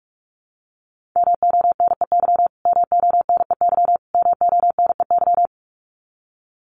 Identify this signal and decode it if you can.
Morse code, heard as a keyed tone, about 44 wpm, 700 Hz. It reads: MONEY MONEY MONEY
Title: Top 500 Words Repeated 45wpm